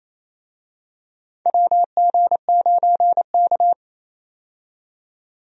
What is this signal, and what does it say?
Morse code, heard as a keyed tone, about 28 wpm, 700 Hz. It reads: WG9K
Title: USA Call Signs - 4 Characters - 28wpm